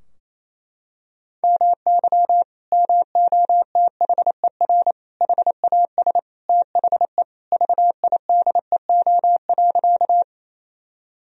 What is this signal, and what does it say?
Morse code, heard as a keyed tone, about 28 wpm, 700 Hz. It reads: MY MOTHER HAS THE VIDEO.
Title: Sentences from Top 1000 Words - Review Part 4 - 28wpm